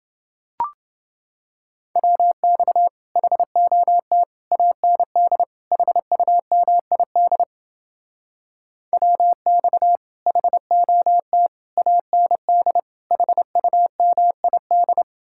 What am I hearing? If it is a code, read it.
Morse code, heard as a keyed tone, about 29 wpm, 700 Hz. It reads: WX HOT AND HUMID WX HOT AND HUMID